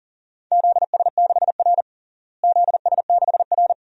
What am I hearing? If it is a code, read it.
Morse code, heard as a keyed tone, about 37 wpm, 700 Hz. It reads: ZS6R ZS6R
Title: DXCC-100 Call Signs - 4 Characters - Repeated 2x - 40wpm